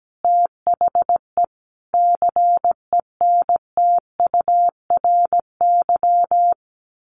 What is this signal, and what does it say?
Morse code, heard as a keyed tone, about 17 wpm, 700 Hz. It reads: THE CENTURY
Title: Sets of 5 Words Encore2 17wpm